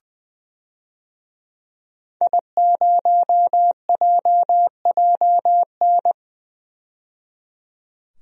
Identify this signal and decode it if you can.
Morse code, heard as a keyed tone, about 20 wpm, 700 Hz. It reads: I0JJN
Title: DXCC-100 Call Signs - 5 Characters - 20wpm